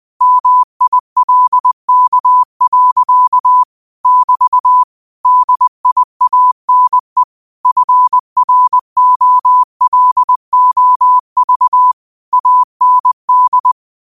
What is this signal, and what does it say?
Morse code, heard as a keyed tone, about 20 wpm, 1000 Hz. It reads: MILK. <BT> DIANE FROLOV AND